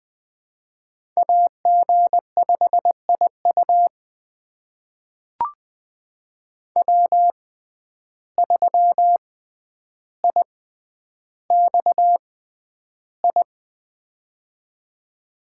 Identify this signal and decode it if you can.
Morse code, heard as a keyed tone, about 20 wpm, spaced as for 10 wpm, 700 Hz. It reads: AG5IU W3IXI
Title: USA Call Signs - 5 Characters - 6x Character Spacing - 20wpm